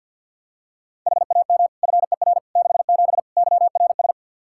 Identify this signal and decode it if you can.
Morse code, heard as a keyed tone, about 52 wpm, 700 Hz. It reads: HAM FER 67 YRS
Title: Practice QSO - Exchange 3 - More Info - 50wpm